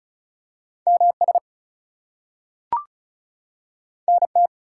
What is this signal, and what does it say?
Morse code, heard as a keyed tone, about 36 wpm, 700 Hz. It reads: MS NT
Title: US States and Canadian Provinces 35wpm